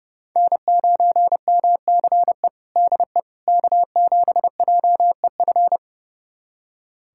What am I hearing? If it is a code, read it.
Morse code, heard as a keyed tone, about 30 wpm, 700 Hz. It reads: N9MCE DE K7JEF
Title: USA Callsign DE Callsign - 5 Characters - 30wpm